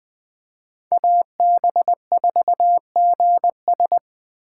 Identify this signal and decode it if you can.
Morse code, heard as a keyed tone, about 24 wpm, 700 Hz. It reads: AB4GS
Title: Speed Racing - USA 2x2 Call Signs 20wpm